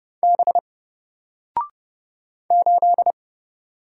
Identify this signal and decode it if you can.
Morse code, heard as a keyed tone, about 30 wpm, 700 Hz. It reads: B 8
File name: Single-Letter-Number-30wpm